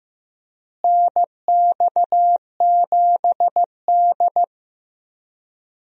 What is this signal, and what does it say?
Morse code, heard as a keyed tone, about 16 wpm, 700 Hz. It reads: NX7D
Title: USA Call Signs - 4 Characters - Repeated 3x - 15wpm